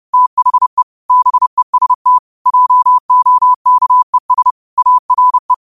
Morse code, heard as a keyed tone, about 30 wpm, 1000 Hz. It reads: THE BEST JOKES ARE
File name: qotdfiles / 30wpm